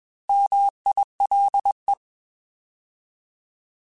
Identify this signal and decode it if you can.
Morse code, heard as a keyed tone, about 21 wpm, 800 Hz. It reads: MILE
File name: Alphabetical / Alphabetical mp3s / Ms @20